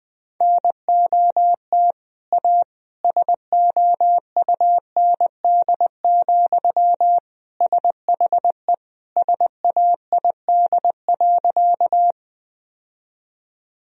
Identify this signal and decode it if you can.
Morse code, heard as a keyed tone, about 20 wpm, 700 Hz. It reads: NOT A SOUND, SHE SAID.
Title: Sentences from Top 1000 Words - Review Part 1 - 20wpm